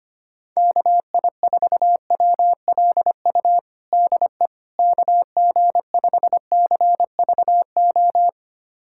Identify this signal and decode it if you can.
Morse code, heard as a keyed tone, about 25 wpm, 700 Hz. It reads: KI4WLU DE KG5CVO
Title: USA Callsign DE Callsign - 6 Characters - 25wpm